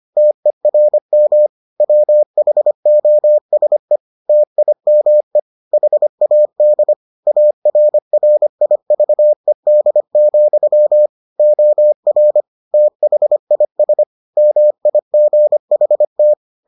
Morse code, heard as a keyed tone, about 25 wpm, 600 Hz. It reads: TERM WHOSE TIME HAD ARRIVED, OR THIS MIGHT